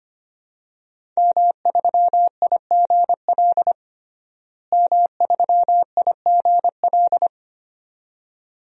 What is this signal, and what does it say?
Morse code, heard as a keyed tone, about 25 wpm, 700 Hz. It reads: M3IGL M3IGL
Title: DXCC-100 Call Signs - 5 Characters - Repeated 2x - 25wpm